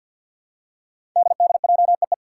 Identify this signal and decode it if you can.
Morse code, heard as a keyed tone, about 51 wpm, 700 Hz. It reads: DDJEE
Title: Five Challenging Letter-Number Combinations 50wpm